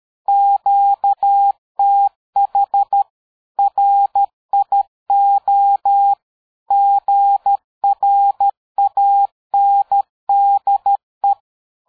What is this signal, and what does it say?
Morse code, heard as a keyed tone, about 13 wpm, 800 Hz. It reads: QTH RIO GRANDE
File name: dk5ke / QSO 60BpM